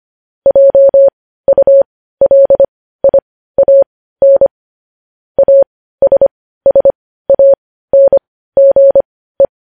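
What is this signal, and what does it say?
Morse code, heard as a keyed tone, about 25 wpm, spaced as for 15 wpm, 550 Hz. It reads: JULIAN ASSANGE
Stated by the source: News headlines sent in Morse code at 15 wpm.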